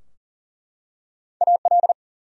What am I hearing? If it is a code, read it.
Morse code, heard as a keyed tone, about 40 wpm, 700 Hz. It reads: AL
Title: USA Common Names - Max 3 Letters - 40wpm